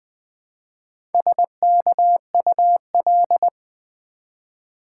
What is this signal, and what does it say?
Morse code, heard as a keyed tone, about 20 wpm, 700 Hz. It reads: SKUL
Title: Build A Word - 1 Letter At A Time - 5-Letter Words - 20wpm